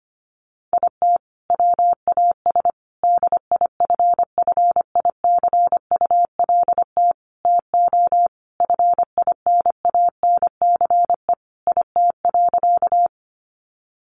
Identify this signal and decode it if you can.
Morse code, heard as a keyed tone, about 25 wpm, 700 Hz. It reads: IT WAS DIFFICULT TO FINANCE IT.
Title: Sentences from Top 1500 Words 25wpm